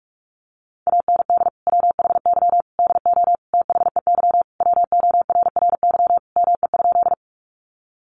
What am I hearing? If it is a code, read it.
Morse code, heard as a keyed tone, about 44 wpm, 700 Hz. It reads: AND WHY DO THEY WORRY ME?
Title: Sentences from Top 1100 Words 45wpm